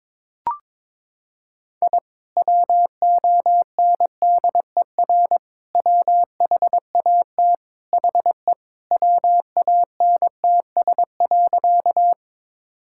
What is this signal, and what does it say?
Morse code, heard as a keyed tone, about 22 wpm, 700 Hz. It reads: I WONDER WHAT HE WANTS.
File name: practice / 22wpm / Sentences with 2-Word Phrases - 22wpm